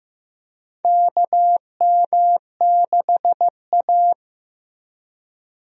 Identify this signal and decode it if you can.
Morse code, heard as a keyed tone, about 16 wpm, 700 Hz. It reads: KM6A
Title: USA Call Signs - 4 Characters - 15wpm